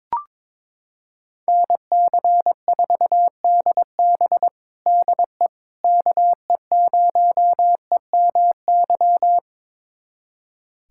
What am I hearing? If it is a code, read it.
Morse code, heard as a keyed tone, about 22 wpm, 700 Hz. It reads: NC4DB DE KE0EMY
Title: USA Callsign DE Callsign 22wpm